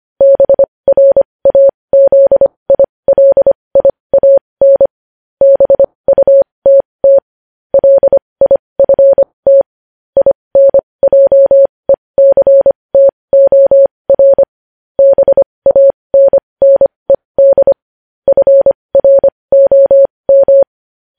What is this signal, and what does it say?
Morse code, heard as a keyed tone, about 25 wpm, spaced as for 20 wpm, 550 Hz. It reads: BRAZILIAN BUTT LIFT INJECTOR BANNED FROM
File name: podcast / News headlines-1x-25x20